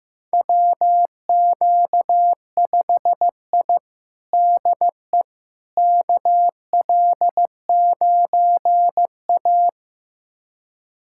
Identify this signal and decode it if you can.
Morse code, heard as a keyed tone, about 16 wpm, 700 Hz. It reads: WQ5I DE KL9A
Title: USA Callsign DE Callsign - 4 Characters - 15wpm